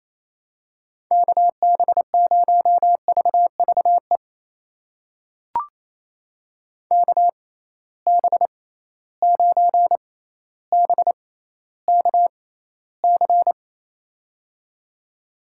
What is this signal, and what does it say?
Morse code, heard as a keyed tone, about 28 wpm, spaced as for 14 wpm, 700 Hz. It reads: KB0VVE KB9BKC
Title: USA Call Signs - 6 Characters - 6x Character Spacing - 28wpm